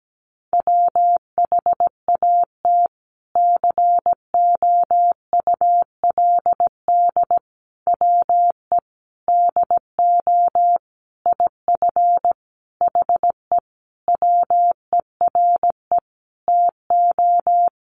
Morse code, heard as a keyed tone, about 17 wpm, 700 Hz. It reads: WHAT COULD WE DO IF HE WERE TO
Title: Sentences from Top 400 Words 17wpm